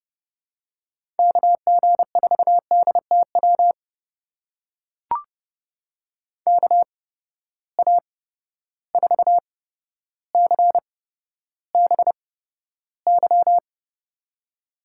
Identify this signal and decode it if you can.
Morse code, heard as a keyed tone, about 31 wpm, spaced as for 13 wpm, 700 Hz. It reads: KG4DTW KA4CBY
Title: USA Call Signs - 6 Characters - 8x Character Spacing - 30wpm